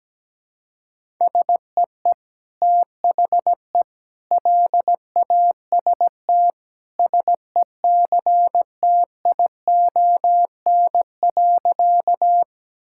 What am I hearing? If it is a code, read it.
Morse code, heard as a keyed tone, about 17 wpm, 700 Hz. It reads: SEE THE LAST SECTION.
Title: Sentences from Top 1000 Words - Review Part 5 - 17wpm